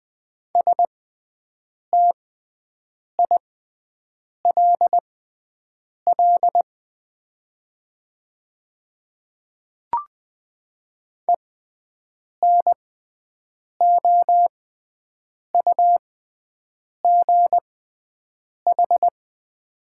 Morse code, heard as a keyed tone, about 20 wpm, spaced as for 7 wpm, 700 Hz. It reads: STILL ENOUGH
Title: 6x Character Spacing - Top 500 Words - 20wpm